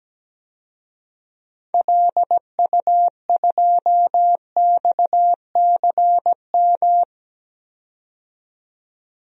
Morse code, heard as a keyed tone, about 17 wpm, 700 Hz. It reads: LU2XCM
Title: DXCC-100 Call Signs - 6 Characters - 17wpm